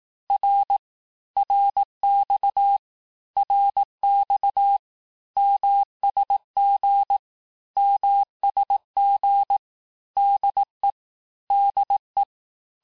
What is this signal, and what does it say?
Morse code, heard as a keyed tone, about 18 wpm, 800 Hz. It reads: R RX RX MSG MSG DE DE
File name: teste abreviations vitesse 17mpm puis 22mpm